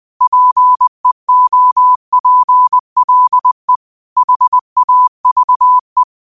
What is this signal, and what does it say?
Morse code, heard as a keyed tone, about 20 wpm, 1000 Hz. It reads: PEOPLE HAVE